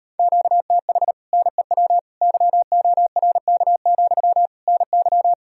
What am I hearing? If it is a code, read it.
Morse code, heard as a keyed tone, about 33 wpm, 700 Hz. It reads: QTH NEW YORK, NY